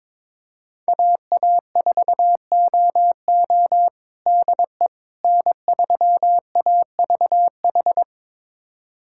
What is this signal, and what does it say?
Morse code, heard as a keyed tone, about 22 wpm, 700 Hz. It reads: AA4OO DE N3AVH
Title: USA Callsign DE Callsign - 5 Characters - 22wpm